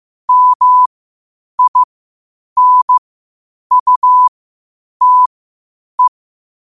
Morse code, heard as a keyed tone, about 15 wpm, spaced as for 8 wpm, 1000 Hz. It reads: MINUTE